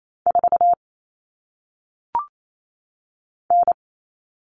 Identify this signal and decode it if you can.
Morse code, heard as a keyed tone, about 28 wpm, 700 Hz. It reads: 4 N
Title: All Cut Numbers - 1-Digit 28wpm